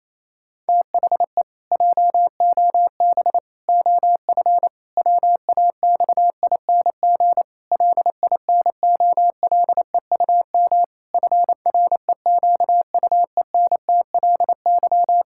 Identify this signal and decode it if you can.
Morse code, heard as a keyed tone, about 28 wpm, 700 Hz. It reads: THE JOB OF WAXING LINOLEUM FREQUENTLY
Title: Pangram Challenge v2 - 28wpm